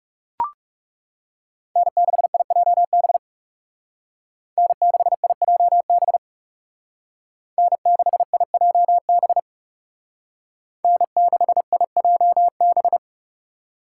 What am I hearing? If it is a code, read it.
Morse code, heard as a keyed tone, about 35 wpm, 700 Hz. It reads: N6IJB N6IJB N6IJB N6IJB
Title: Speed Racing - USA 1x3 Call Signs 30wpm